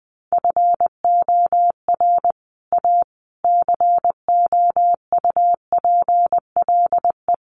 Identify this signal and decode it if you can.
Morse code, heard as a keyed tone, about 20 wpm, 700 Hz. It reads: FOR A COUPLE
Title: Sets of 5 Words Encore2 20wpm